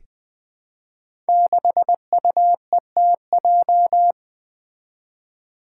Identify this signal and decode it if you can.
Morse code, heard as a keyed tone, about 20 wpm, 700 Hz. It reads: 6UETJ
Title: Five Challenging Letter-Number Combinations 20wpm